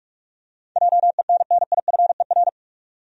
Morse code, heard as a keyed tone, about 46 wpm, 700 Hz. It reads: JENNIFER
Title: USA Common Names - 45wpm